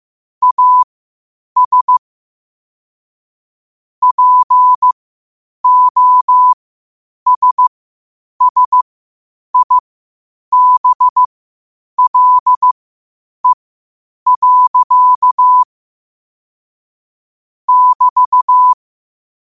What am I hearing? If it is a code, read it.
Morse code, heard as a keyed tone, about 15 wpm, spaced as for 8 wpm, 1000 Hz. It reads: AS POSSIBLE. <BT>
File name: qotdfiles / 8wpm